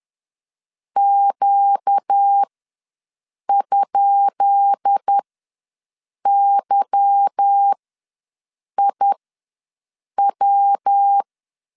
Morse code, heard as a keyed tone, about 10 wpm, spaced as for 6 wpm, 800 Hz. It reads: Q?YIW
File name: LEZIONI / 10 SIGNS / 10B INT ws 10wpm 4134